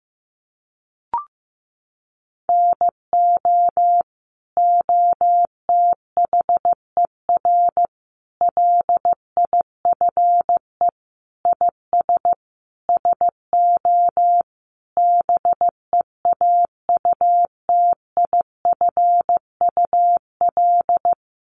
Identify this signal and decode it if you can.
Morse code, heard as a keyed tone, about 15 wpm, 700 Hz. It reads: NO OTHER LIFE IS SO BEAUTIFUL
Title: Sentences from Top 1500 Words 15wpm